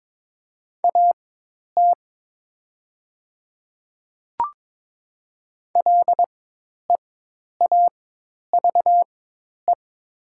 Morse code, heard as a keyed tone, about 22 wpm, spaced as for 10 wpm, 700 Hz. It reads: AT LEAVE
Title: 4x Character Spacing - Top 500 Words - 22wpm